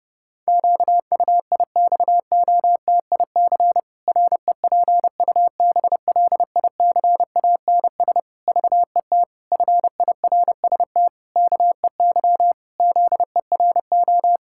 Morse code, heard as a keyed tone, about 32 wpm, 700 Hz. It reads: QUIXOTIC REPUBLICANS VET FIRST KEY ZERO
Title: Pangram Challenge v2 - 30wpm